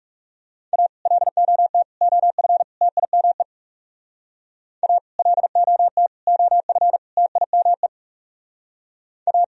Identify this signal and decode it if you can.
Morse code, heard as a keyed tone, about 37 wpm, 700 Hz. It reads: A LOT OF TIME A LOT OF TIME A
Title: Speed-Racing - Sets of 4 Words 30wpm